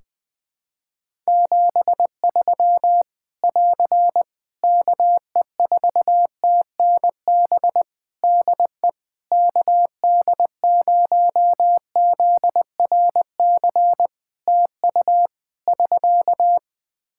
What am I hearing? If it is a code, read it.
Morse code, heard as a keyed tone, about 20 wpm, 700 Hz. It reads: 73 <AR> KE4TNB DE KD0ZRC TU <SK>
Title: K1USN SST QSO Practice 20wpm